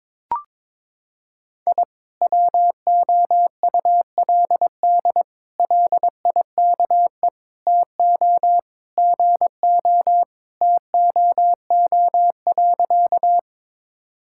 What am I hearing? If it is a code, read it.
Morse code, heard as a keyed tone, about 22 wpm, 700 Hz. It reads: I WOULD LIKE TO GO TOO.
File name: practice / 22wpm / Sentences with 3-Word Phrases - 22wpm